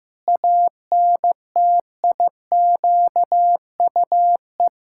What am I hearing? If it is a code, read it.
Morse code, heard as a keyed tone, about 15 wpm, 700 Hz. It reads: ANTIQUE
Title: Pangram Challenge v2 - 15wpm